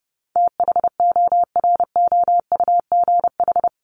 Morse code, heard as a keyed tone, about 30 wpm, 700 Hz. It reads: THOROUGH
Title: LY Suffix - Top 100 Words with Base Words 30wpm